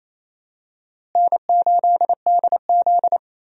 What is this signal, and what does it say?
Morse code, heard as a keyed tone, about 27 wpm, 700 Hz. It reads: N8DZ
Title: USA Call Signs - 4 Characters - Repeated 2x - 28wpm